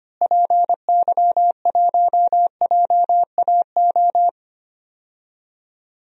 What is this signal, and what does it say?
Morse code, heard as a keyed tone, about 25 wpm, 700 Hz. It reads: PY1JAO
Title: DXCC-100 Call Signs - 6 Characters - Repeated 2x - 25wpm